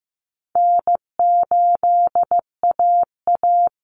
Morse code, heard as a keyed tone, about 15 wpm, 700 Hz. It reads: N8AA
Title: Call Signs Repeated 2x - 15wpm